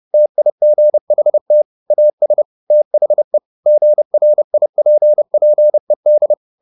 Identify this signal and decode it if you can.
Morse code, heard as a keyed tone, about 30 wpm, 600 Hz. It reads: TIGHT AS THE GRIPPED
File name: the old man and the sea / the old man and the sea 1 30wpm